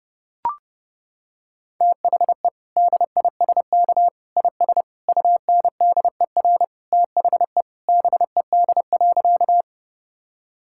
Morse code, heard as a keyed tone, about 32 wpm, 700 Hz. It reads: THE DISK IS UNDER THE BED.
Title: Sentences from Top 1000 Words - Review Part 6 - 30wpm